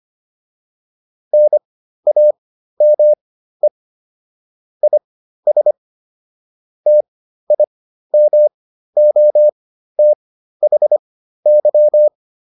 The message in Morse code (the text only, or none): NAME IS TIMOTHY